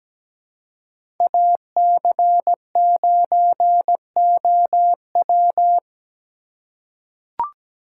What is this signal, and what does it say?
Morse code, heard as a keyed tone, about 17 wpm, 700 Hz. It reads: AC9OW
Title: USA Call Signs - 5 Characters - 3x Character Spacing - 17wpm